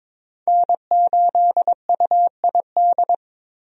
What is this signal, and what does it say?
Morse code, heard as a keyed tone, about 22 wpm, 700 Hz. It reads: N8UID
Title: USA Call Signs - 5 Characters - Repeated 2x - 22wpm